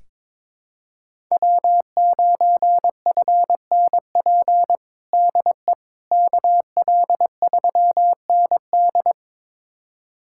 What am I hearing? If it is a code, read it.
Morse code, heard as a keyed tone, about 22 wpm, 700 Hz. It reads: W9FNP DE KL3ND
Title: USA Callsign DE Callsign - 5 Characters - 22wpm